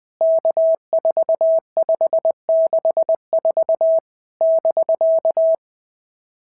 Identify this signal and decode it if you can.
Morse code, heard as a keyed tone, about 20 wpm, 650 Hz. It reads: K4564 <BK>
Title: POTA QSO Practice - 20wpm